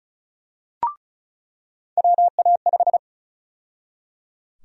Morse code, heard as a keyed tone, about 34 wpm, 700 Hz. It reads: WA5
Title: USA Call Sign Prefixes - 35wpm